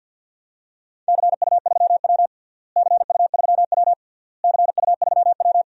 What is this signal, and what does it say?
Morse code, heard as a keyed tone, about 49 wpm, 700 Hz. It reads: CU3W CU3W CU3W
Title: DXCC-100 Call Signs - 4 Characters - Repeated 3x - 50wpm